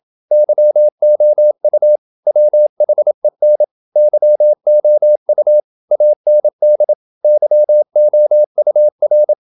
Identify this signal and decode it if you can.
Morse code, heard as a keyed tone, about 27 wpm, 600 Hz. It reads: YOU WHEN YOU AND YOUR
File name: Never 0027wpm